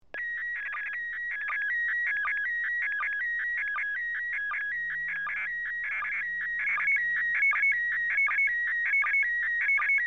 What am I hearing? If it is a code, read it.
SSTV (slow-scan television), PD 180.
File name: iss-sstv